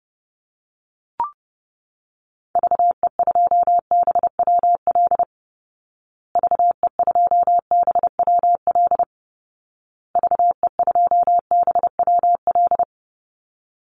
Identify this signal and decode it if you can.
Morse code, heard as a keyed tone, about 30 wpm, 700 Hz. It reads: VE2BWL VE2BWL VE2BWL
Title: Call Signs Repeated 2x - 30wpm